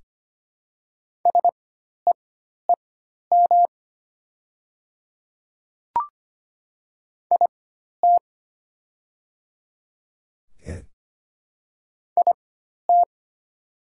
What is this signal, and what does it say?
Morse code, heard as a keyed tone, about 25 wpm, spaced as for 12 wpm, 700 Hz. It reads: SEEM IT IT
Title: 4x Character Spacing - Top 300 Words - 25wpm